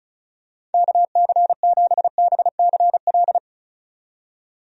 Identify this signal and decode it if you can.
Morse code, heard as a keyed tone, about 36 wpm, 700 Hz. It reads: KC7BCL
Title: USA Call Signs - 6 Characters - 35wpm